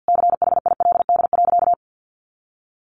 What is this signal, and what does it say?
Morse code, heard as a keyed tone, about 50 wpm, 700 Hz. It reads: CHILD.
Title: Sentences from Top 1300 Words 50wpm